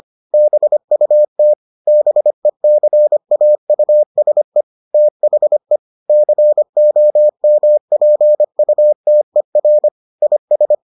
Morse code, heard as a keyed tone, about 25 wpm, 600 Hz. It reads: BUT BECAUSE THE COMPUTER IS